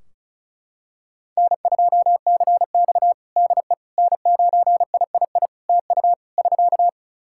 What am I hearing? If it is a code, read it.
Morse code, heard as a keyed tone, about 34 wpm, 700 Hz. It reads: N2CX DE N9III TU <SK>
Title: K1USN SST QSO Practice 35wpm